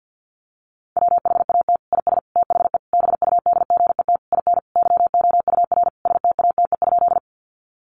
Morse code, heard as a keyed tone, about 50 wpm, 700 Hz. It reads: WHAT IS THE BUDGET IN YOUR STATE?
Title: Sentences from Top 1200 Words 50wpm